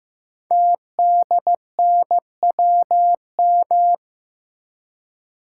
Morse code, heard as a keyed tone, about 15 wpm, 700 Hz. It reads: TDNWM
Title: Code Groups - 5 Letters - 15wpm